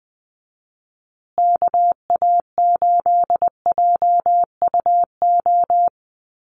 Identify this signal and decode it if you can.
Morse code, heard as a keyed tone, about 20 wpm, 700 Hz. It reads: KA8JUO
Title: USA Call Signs Repeated 2x 20wpm